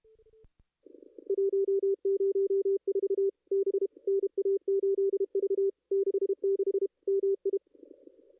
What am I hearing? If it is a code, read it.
Morse code, heard as a keyed tone, about 30 wpm, 440 Hz. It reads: 104 B NA8V 66 MI
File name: contests / ss-cw-k5tr-op-n5rz -2200